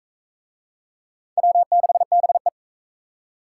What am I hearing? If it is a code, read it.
Morse code, heard as a keyed tone, about 42 wpm, 700 Hz. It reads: W6BE